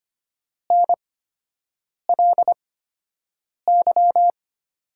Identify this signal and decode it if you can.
Morse code, heard as a keyed tone, about 25 wpm, spaced as for 7 wpm, 700 Hz. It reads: NLY
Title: 8x Character Spacing - Top 200 Words - 25wpm